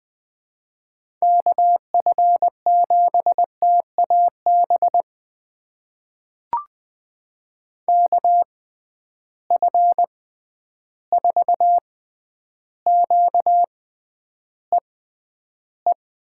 Morse code, heard as a keyed tone, about 20 wpm, spaced as for 10 wpm, 700 Hz. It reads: KF7TAB KF4QEE
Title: USA Call Signs - 6 Characters - 6x Character Spacing - 20wpm